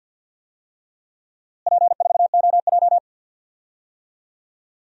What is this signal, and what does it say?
Morse code, heard as a keyed tone, about 51 wpm, 700 Hz. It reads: P4OJ
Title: DXCC-100 Call Signs - 4 Characters - Repeated 3x - 50wpm